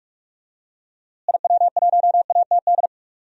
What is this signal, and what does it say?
Morse code, heard as a keyed tone, about 46 wpm, 700 Hz. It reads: IW1ATD
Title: DXCC-100 Call Signs - 6 Characters - Repeated 3x - 45wpm